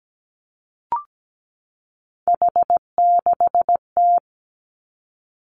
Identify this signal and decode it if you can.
Morse code, heard as a keyed tone, about 17 wpm, 700 Hz. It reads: H6T
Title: Three Challenging Letter-Number Combinations 17wpm